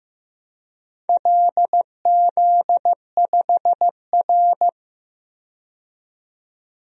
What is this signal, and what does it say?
Morse code, heard as a keyed tone, about 15 wpm, 700 Hz. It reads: LZ5R
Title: DXCC-100 Call Signs - 4 Characters - 15wpm